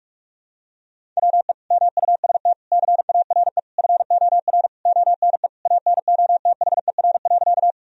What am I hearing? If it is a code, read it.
Morse code, heard as a keyed tone, about 46 wpm, 700 Hz. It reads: WE MUST CARE FOR ONE ANOTHER.
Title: Sentences from Top 1000 Words - Review Part 1 - 45wpm